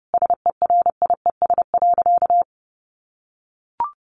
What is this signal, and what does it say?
Morse code, heard as a keyed tone, about 30 wpm, 700 Hz. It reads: SERIES.
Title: Sentences from Top 1200 Words 30wpm